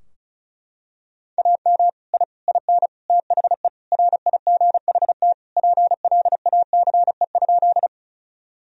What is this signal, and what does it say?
Morse code, heard as a keyed tone, about 36 wpm, 700 Hz. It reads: AM I IN THE RIGHT PLACE?
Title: Sentences from Top 1000 Words - Review Part 2 - 35wpm